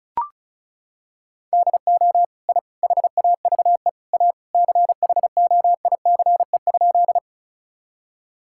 Morse code, heard as a keyed tone, about 36 wpm, 700 Hz. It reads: DO I HAVE A CHOICE?
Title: Sentences from Top 1000 Words - Review Part 6 - 35wpm